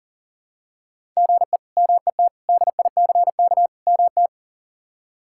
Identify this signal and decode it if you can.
Morse code, heard as a keyed tone, about 38 wpm, 700 Hz. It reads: GE MET DICK MT
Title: K1USN SST QSO Practice - 40wpm